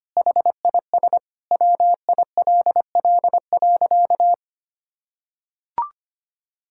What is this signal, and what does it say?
Morse code, heard as a keyed tone, about 25 wpm, 700 Hz. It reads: HIS WILL.
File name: practice / 25wpm / Sentences with 9-Word Phrases - 25wpm